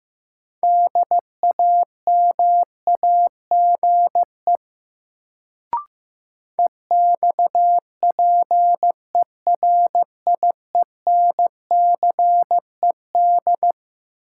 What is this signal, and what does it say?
Morse code, heard as a keyed tone, about 15 wpm, 700 Hz. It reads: DAMAGE EXPERIENCED
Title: Sentences from Top 1100 Words - New Word Drill - 15wpm